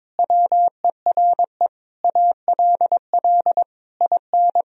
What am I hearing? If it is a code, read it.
Morse code, heard as a keyed tone, about 22 wpm, 700 Hz. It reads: WERE ALL IN
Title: Sentences from Top 1000 Words - Review Part 1 - 22wpm